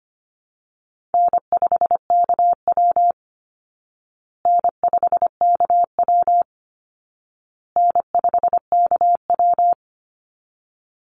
Morse code, heard as a keyed tone, about 25 wpm, 700 Hz. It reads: N5KW N5KW N5KW
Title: Easy Call Signs Repeated 3x - 25wpm